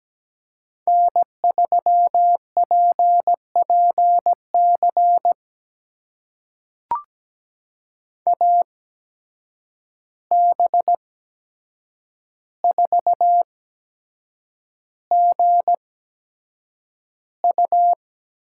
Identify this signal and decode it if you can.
Morse code, heard as a keyed tone, about 17 wpm, spaced as for 7 wpm, 700 Hz. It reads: N3PPC AB4GU
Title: USA Call Signs - 5 Characters - 8x Character Spacing - 17wpm